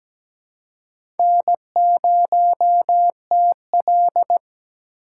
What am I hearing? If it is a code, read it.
Morse code, heard as a keyed tone, about 17 wpm, 700 Hz. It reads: N0TL
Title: USA Call Signs - 4 Characters - 4x Character Spacing - 17wpm